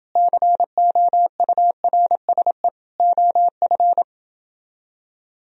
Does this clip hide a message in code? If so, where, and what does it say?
Yes: Morse code from 0.2 to 4.0 s, heard as a keyed tone, about 29 wpm, 700 Hz. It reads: COURSE OF